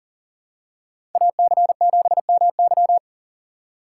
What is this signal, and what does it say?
Morse code, heard as a keyed tone, about 37 wpm, 700 Hz. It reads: AC7MY
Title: USA Call Signs - 2x Character Spacing - 40wpm